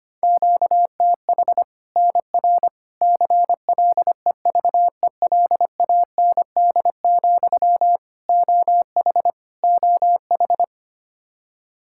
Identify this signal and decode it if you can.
Morse code, heard as a keyed tone, about 25 wpm, 700 Hz. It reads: QTH NR CLEVELAND, OH OH
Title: Practice QSO - Exchange 1 - Beginning - 25wpm